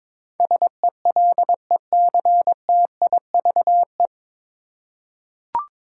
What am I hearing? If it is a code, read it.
Morse code, heard as a keyed tone, about 22 wpm, 700 Hz. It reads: SELECTIVE
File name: 9 Letter Words - First 3 Letters Given - 22wpm